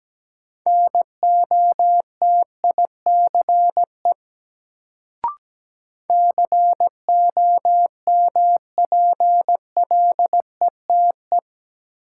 Morse code, heard as a keyed tone, about 17 wpm, 700 Hz. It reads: NOTICE COMPLETE
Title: Sentences from Top 400 Words - New Word Drill - 17wpm